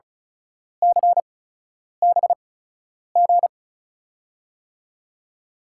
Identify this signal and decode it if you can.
Morse code, heard as a keyed tone, about 36 wpm, spaced as for 10 wpm, 700 Hz. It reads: CBG
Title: USA Call Signs - 8x Character Spacing - 35wpm